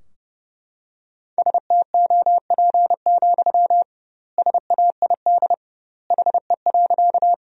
Morse code, heard as a keyed tone, about 32 wpm, 700 Hz. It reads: STOP, SAID HE.
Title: 2x Word-Spacing - Sentences from Top 300 Words - 30wpm